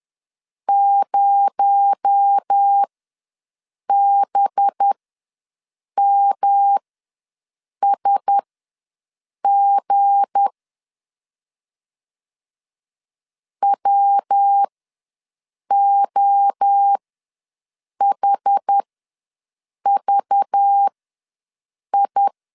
Morse code, heard as a keyed tone, about 10 wpm, spaced as for 6 wpm, 800 Hz. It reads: 0BMSG WOHVI
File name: LEZIONI / 7 GF9Z / 7 GF9 ws 10wpm 4034